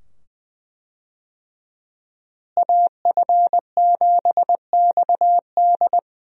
Morse code, heard as a keyed tone, about 20 wpm, 700 Hz. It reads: AF7XD
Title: USA Call Signs - 5 Characters - Repeated 2x - 20wpm